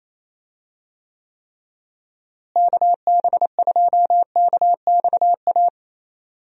Morse code, heard as a keyed tone, about 27 wpm, 700 Hz. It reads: KB2KXA
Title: USA Call Signs - 6 Characters - Repeated 3x - 28wpm